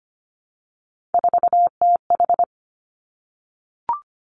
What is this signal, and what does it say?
Morse code, heard as a keyed tone, about 25 wpm, 700 Hz. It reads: 4TH
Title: Three Challenging Letter-Number Combinations 25wpm